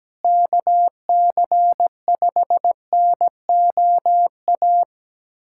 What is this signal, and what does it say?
Morse code, heard as a keyed tone, about 17 wpm, 700 Hz. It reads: KC5NOA
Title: USA Callsign DE Callsign 17wpm